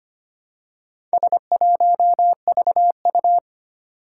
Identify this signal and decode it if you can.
Morse code, heard as a keyed tone, about 25 wpm, 700 Hz. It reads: S1VU
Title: Four Challenging Letter-Number Combinations 25wpm